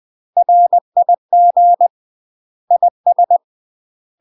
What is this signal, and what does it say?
Morse code, heard as a keyed tone, about 20 wpm, 700 Hz. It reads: RIG IS
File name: UK MASTER QSOs TRAINING/20WPM